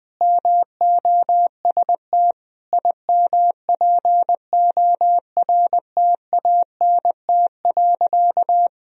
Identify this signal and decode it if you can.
Morse code, heard as a keyed tone, about 21 wpm, 700 Hz. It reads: MOST IMPORTANT.
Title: Sentences from Top 1000 Words - Review Part 2 - 20wpm